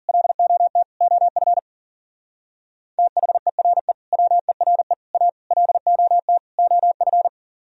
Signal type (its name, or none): Morse code